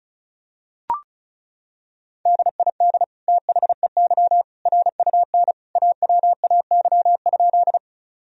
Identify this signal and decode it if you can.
Morse code, heard as a keyed tone, about 36 wpm, 700 Hz. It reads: DID THEY RUN AWAY?
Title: Sentences from Top 1000 Words - Review Part 1 - 35wpm